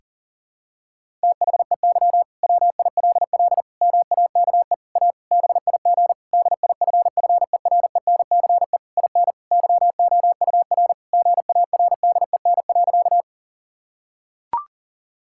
Morse code, heard as a keyed tone, about 40 wpm, 700 Hz. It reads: THEY WILL MAKE A BIG DIFFERENCE IN YOUR GARDEN.
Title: Sentences from Top 1000 Words - Review Part 5 - 40wpm